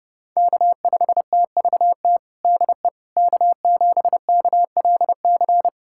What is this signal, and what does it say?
Morse code, heard as a keyed tone, about 30 wpm, 700 Hz. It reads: K5TVT DE K7KLC
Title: USA Callsign DE Callsign - 5 Characters - 30wpm